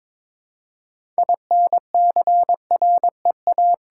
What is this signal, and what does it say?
Morse code, heard as a keyed tone, about 22 wpm, 700 Hz. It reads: INCREA
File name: practice / 22wpm / Build A Phrase - 5-Word Phrases - 22wpm